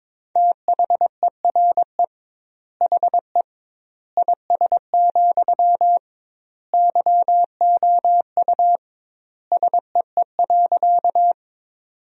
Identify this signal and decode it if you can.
Morse code, heard as a keyed tone, about 22 wpm, 700 Hz. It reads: THERE HE IS, YOU SEE.
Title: 2x Word-Spacing - Sentences from Top 100 Words - 22wpm